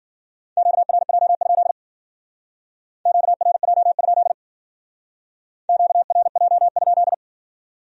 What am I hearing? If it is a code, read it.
Morse code, heard as a keyed tone, about 54 wpm, 700 Hz. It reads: QRJ? QRJ? QRJ?